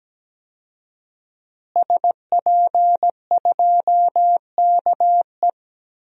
Morse code, heard as a keyed tone, about 17 wpm, 700 Hz. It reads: SP2KE
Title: DXCC-100 Call Signs - 5 Characters - Repeated 3x - 17wpm